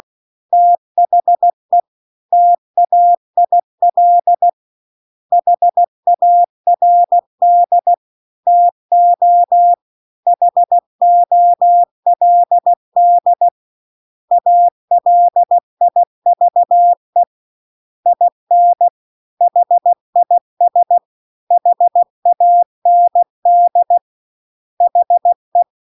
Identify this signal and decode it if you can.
Morse code, heard as a keyed tone, about 16 wpm, 700 Hz. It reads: THE TAIL HARD TO HOLD ALIVE IN HIS HAND HE